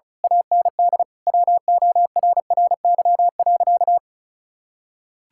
Morse code, heard as a keyed tone, about 36 wpm, 700 Hz. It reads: AND WORRY.
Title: Pangram Challenge v2 - 35wpm